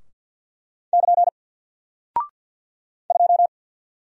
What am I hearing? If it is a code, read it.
Morse code, heard as a keyed tone, about 50 wpm, 700 Hz. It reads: <KN> 2
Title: Single Character (Letter:Number:Punctuation:Prosign) Alt 50wpm